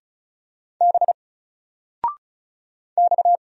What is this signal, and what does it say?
Morse code, heard as a keyed tone, about 33 wpm, 700 Hz. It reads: B X
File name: Single-Letter-Number-35wpm